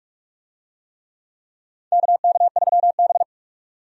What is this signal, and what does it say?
Morse code, heard as a keyed tone, about 41 wpm, 700 Hz. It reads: KK3B
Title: USA Call Signs - 4 Characters - 45wpm